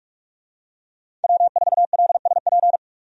Morse code, heard as a keyed tone, about 47 wpm, 700 Hz. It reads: W4LSP
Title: USA Call Signs - 5 Characters - 45wpm